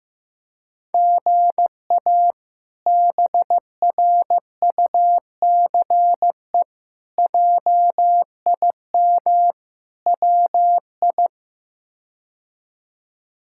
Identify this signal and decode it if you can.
Morse code, heard as a keyed tone, about 15 wpm, 700 Hz. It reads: GA BRUCE JIM WI
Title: K1USN SST QSO Practice - 15wpm